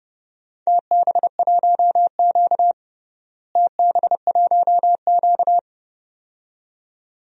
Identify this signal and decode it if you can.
Morse code, heard as a keyed tone, about 30 wpm, 700 Hz. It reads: TB1Q TB1Q
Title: DXCC-100 Call Signs - 4 Characters - Repeated 2x - 30wpm